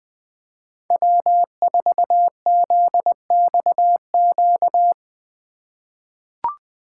Morse code, heard as a keyed tone, about 20 wpm, 700 Hz. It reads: W4ZXQ
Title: USA Call Signs - 5 Characters - 2x Character Spacing - 20wpm